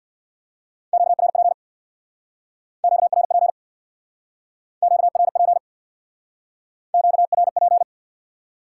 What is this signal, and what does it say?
Morse code, heard as a keyed tone, about 67 wpm, 700 Hz. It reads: QRP QRP QRP QRP